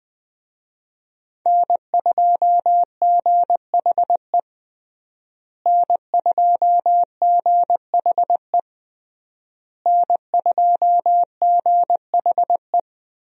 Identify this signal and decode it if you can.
Morse code, heard as a keyed tone, about 20 wpm, 700 Hz. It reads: N2GHE N2GHE N2GHE
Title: USA Call Signs - 5 Characters - Repeated 3x - 20wpm